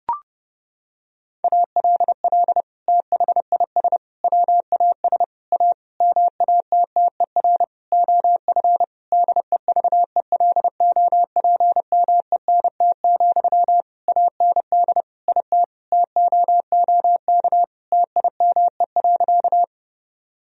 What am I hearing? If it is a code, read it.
Morse code, heard as a keyed tone, about 30 wpm, 700 Hz. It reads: ALL THIS WAS A MATTER OF DEVELOPMENT, AND IT TOOK TIME.
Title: Sentences from Top 1000 Words - Review Part 4 - 30wpm